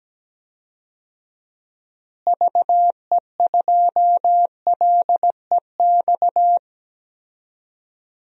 Morse code, heard as a keyed tone, about 17 wpm, 700 Hz. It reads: VE2LEX
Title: DXCC-100 Call Signs - 6 Characters - Repeated 2x - 17wpm